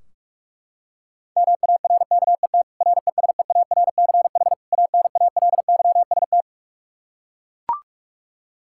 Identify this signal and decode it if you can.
Morse code, heard as a keyed tone, about 41 wpm, 700 Hz. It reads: MARKET RESEARCH ANALYST
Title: Practice QSO - Exchange 3 - More Info - 45wpm